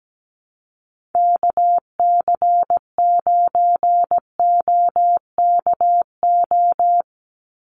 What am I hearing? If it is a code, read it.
Morse code, heard as a keyed tone, about 17 wpm, 700 Hz. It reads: KC9OKO
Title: USA Call Signs 17wpm